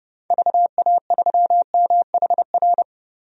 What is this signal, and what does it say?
Morse code, heard as a keyed tone, about 30 wpm, 700 Hz. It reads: VA3MHR
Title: DXCC-100 Call Signs - 6 Characters - 30wpm